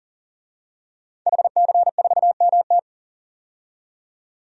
Morse code, heard as a keyed tone, about 40 wpm, 700 Hz. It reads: HC4MT
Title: DXCC-100 Call Signs - 5 Characters - 40wpm